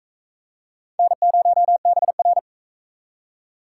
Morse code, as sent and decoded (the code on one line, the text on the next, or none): -. ----- -... .-.
N0BR